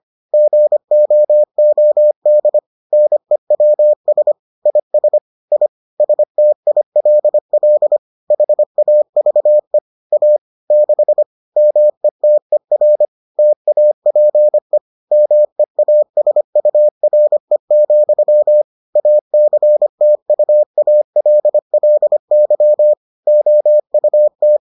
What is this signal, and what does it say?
Morse code, heard as a keyed tone, about 25 wpm, 600 Hz. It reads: GOOD NEWS IS I STILL HAVE A 6 METER TAPE MEASURE, ACTUALLY OUT